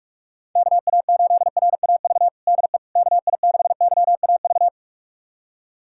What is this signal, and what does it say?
Morse code, heard as a keyed tone, about 46 wpm, 700 Hz. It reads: KA8RAV DE KI6YAV
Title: USA Callsign DE Callsign - 6 Characters - 45wpm